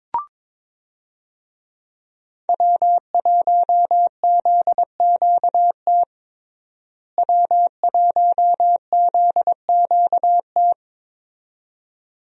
Morse code, heard as a keyed tone, about 22 wpm, 700 Hz. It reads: W1ZQT W1ZQT
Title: USA Call Signs - 5 Characters - Repeated 2x - 22wpm